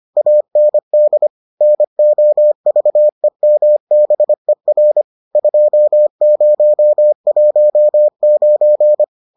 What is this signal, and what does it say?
Morse code, heard as a keyed tone, about 25 wpm, 600 Hz. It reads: AND NOVEMBER 2019